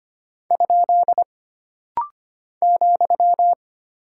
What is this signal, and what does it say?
Morse code, heard as a keyed tone, about 25 wpm, 700 Hz. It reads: ? ,
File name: Single Punctuation-Prosign - Rapid-Fire - 25wpm